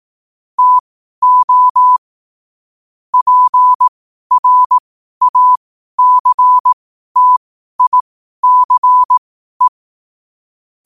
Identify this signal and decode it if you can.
Morse code, heard as a keyed tone, about 18 wpm, spaced as for 12 wpm, 1000 Hz. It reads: TO PRACTICE
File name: 2025-11-08-12wpm